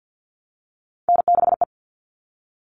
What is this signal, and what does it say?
Morse code, heard as a keyed tone, about 50 wpm, 700 Hz. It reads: N6E
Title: All Cut Numbers - 3-Digit 50wpm